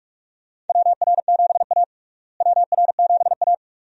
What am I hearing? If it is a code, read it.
Morse code, heard as a keyed tone, about 46 wpm, 700 Hz. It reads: WR7A WR7A
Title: USA Call Signs - 4 Characters - Repeated 2x - 45wpm